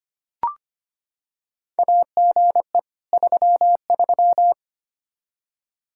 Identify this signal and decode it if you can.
Morse code, heard as a keyed tone, about 25 wpm, 700 Hz. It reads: AGE 33
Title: Practice QSO - Exchange 3 - More Info - 25wpm